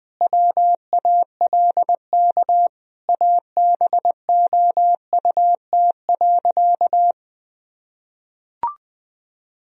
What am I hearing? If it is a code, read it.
Morse code, heard as a keyed tone, about 20 wpm, 700 Hz. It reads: WALK ABOUT.
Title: Sentences from Top 1000 Words - Review Part 1 - 20wpm